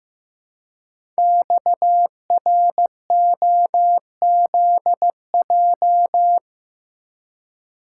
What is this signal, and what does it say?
Morse code, heard as a keyed tone, about 15 wpm, 690 Hz. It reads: XROZJ
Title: Code Groups - 5 Letters - 15wpm